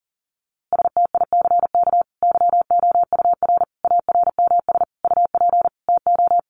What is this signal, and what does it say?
Morse code, heard as a keyed tone, about 40 wpm, 700 Hz. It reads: STICK YOUR ARMS UP TO
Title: A Sound Of Thunder 40wpm